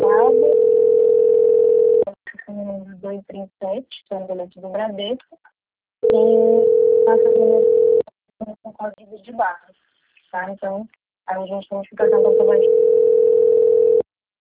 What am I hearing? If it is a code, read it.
Ringback tone (North America), 440 + 480 Hz, with speech.
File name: rg-1600-unknown- - - .5646